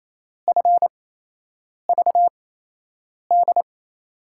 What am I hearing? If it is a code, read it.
Morse code, heard as a keyed tone, about 28 wpm, spaced as for 8 wpm, 700 Hz. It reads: FVD
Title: USA Call Signs - 6 Characters - 8x Character Spacing - 28wpm